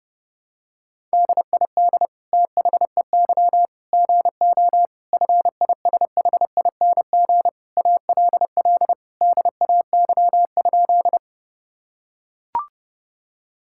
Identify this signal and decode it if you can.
Morse code, heard as a keyed tone, about 30 wpm, 700 Hz. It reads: DID THEY GO FISHING ALL DAY?
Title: Sentences from Top 1000 Words - Review Part 5 - 30wpm